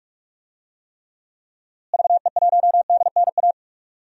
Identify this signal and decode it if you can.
Morse code, heard as a keyed tone, about 44 wpm, 700 Hz. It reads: VE1DNA
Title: DXCC-100 Call Signs - 6 Characters - Repeated 3x - 45wpm